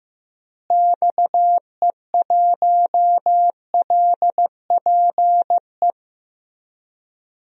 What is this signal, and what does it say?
Morse code, heard as a keyed tone, about 16 wpm, 700 Hz. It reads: XE1LPE
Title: DXCC-100 Call Signs - 6 Characters - Repeated 3x - 15wpm